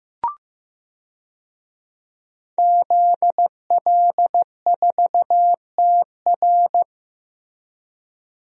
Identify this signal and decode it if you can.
Morse code, heard as a keyed tone, about 15 wpm, 700 Hz. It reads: ZL4TR
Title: DXCC-100 Call Signs - 5 Characters - Repeated 2x - 15wpm